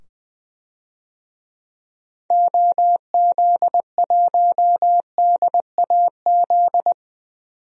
Morse code, heard as a keyed tone, about 20 wpm, 700 Hz. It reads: OZ1DAZ
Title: DXCC-100 Call Signs - 6 Characters - 20wpm